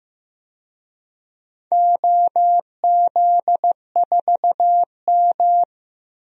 Morse code, heard as a keyed tone, about 15 wpm, 700 Hz. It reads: OZ4M
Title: DXCC-100 Call Signs - 4 Characters - 15wpm